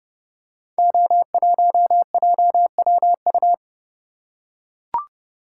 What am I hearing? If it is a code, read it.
Morse code, heard as a keyed tone, about 30 wpm, 700 Hz. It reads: O1JWU
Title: Code Groups - 5 Letter:Numbers - 30wpm